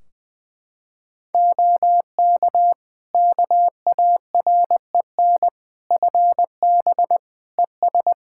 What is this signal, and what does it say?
Morse code, heard as a keyed tone, about 21 wpm, 700 Hz. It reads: OK KAREN FB ES
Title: Practice QSO - Exchange 2 - Info - 20wpm